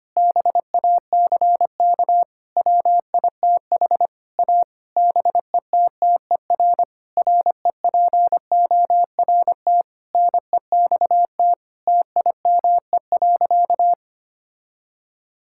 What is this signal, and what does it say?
Morse code, heard as a keyed tone, about 25 wpm, 700 Hz. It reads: BACK WITH A BETTER REPORT NEXT TIME.
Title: Sentences from Top 1000 Words - Review Part 6 - 25wpm